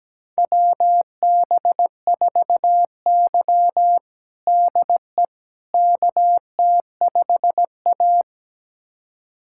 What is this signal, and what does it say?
Morse code, heard as a keyed tone, about 17 wpm, 700 Hz. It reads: WB4Y DE KT5A
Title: USA Callsign DE Callsign - 4 Characters - 17wpm